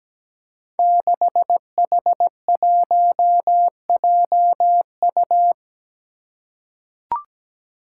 Morse code, heard as a keyed tone, about 17 wpm, 700 Hz. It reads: 6H1JU
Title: Five Challenging Letter-Number Combinations 17wpm